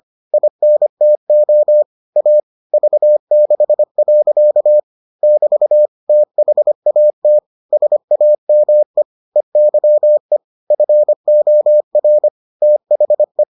Morse code, heard as a keyed tone, about 25 wpm, 600 Hz. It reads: INTO A V6. <BT> THAT SAME EYE FOR THE